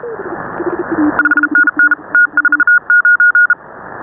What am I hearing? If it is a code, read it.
Morse code, heard as a keyed tone, about 36 wpm, 1440 Hz. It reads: 5NN T49